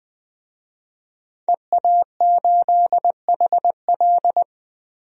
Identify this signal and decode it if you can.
Morse code, heard as a keyed tone, about 20 wpm, 700 Hz. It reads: EA8HL
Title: DXCC-100 Call Signs - 5 Characters - Repeated 2x - 20wpm